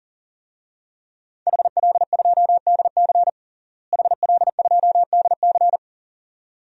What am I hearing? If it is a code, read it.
Morse code, heard as a keyed tone, about 39 wpm, 700 Hz. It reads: HL2DC HL2DC
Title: DXCC-100 Call Signs - 5 Characters - Repeated 2x - 40wpm